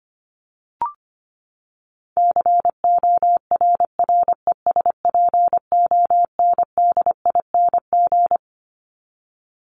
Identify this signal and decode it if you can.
Morse code, heard as a keyed tone, about 25 wpm, 700 Hz. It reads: CORRESPONDING
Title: ING Suffix - Top 100 Words with Base Words 25wpm